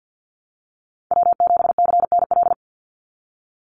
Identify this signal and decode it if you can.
Morse code, heard as a keyed tone, about 50 wpm, 700 Hz. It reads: W7CNL
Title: NA SOTA Chaser Callsigns 50wpm